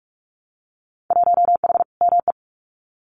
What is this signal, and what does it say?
Morse code, heard as a keyed tone, about 46 wpm, 700 Hz. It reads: 1H ME
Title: Winter Field Day Exchanges Encore 45wpm